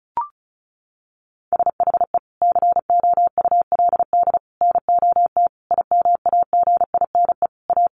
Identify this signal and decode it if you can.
Morse code, heard as a keyed tone, about 34 wpm, 700 Hz. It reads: SHE COULD NOT IMAGINE A
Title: Sentences from Top 1400 Words 35wpm